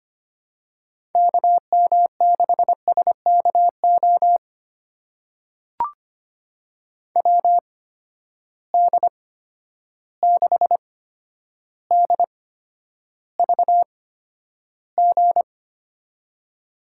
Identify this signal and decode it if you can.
Morse code, heard as a keyed tone, about 25 wpm, spaced as for 11 wpm, 700 Hz. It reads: KM6SKO WD6DVG
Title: USA Call Signs - 6 Characters - 8x Character Spacing - 25wpm